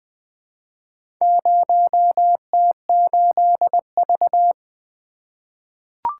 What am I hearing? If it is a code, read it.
Morse code, heard as a keyed tone, about 20 wpm, 700 Hz. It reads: 0T8V
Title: Four Letter-Number Combinations - 20wpm